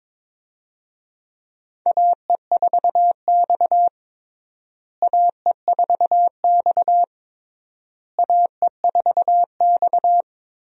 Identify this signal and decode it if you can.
Morse code, heard as a keyed tone, about 22 wpm, 700 Hz. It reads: AE4X AE4X AE4X
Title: USA Call Signs - 4 Characters - Repeated 3x - 22wpm